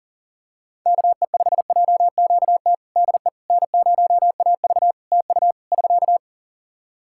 Morse code, heard as a keyed tone, about 40 wpm, 700 Hz. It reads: KE5JQT DE N0AV TU <SK>
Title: Practice QSO - Exchange 4 - Ending - 40wpm